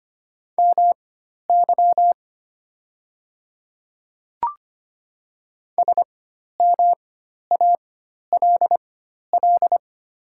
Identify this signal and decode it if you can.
Morse code, heard as a keyed tone, about 25 wpm, spaced as for 12 wpm, 690 Hz. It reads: MY SMALL
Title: 4x Character Spacing - Top 200 Words - 25wpm